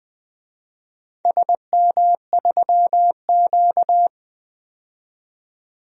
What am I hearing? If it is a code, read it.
Morse code, heard as a keyed tone, about 20 wpm, 700 Hz. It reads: SM3Q
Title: DXCC-100 Call Signs - 4 Characters - 20wpm